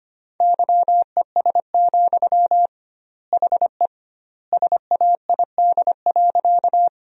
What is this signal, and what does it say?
Morse code, heard as a keyed tone, about 25 wpm, 700 Hz. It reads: YES, HE SAID.
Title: 2x Word-Spacing - Sentences from Top 500 Words - 25wpm